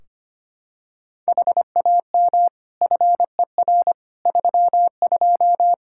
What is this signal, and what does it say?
Morse code, heard as a keyed tone, about 25 wpm, 700 Hz. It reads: HAM FER 32
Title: Practice QSO - Exchange 3 - More Info - 25wpm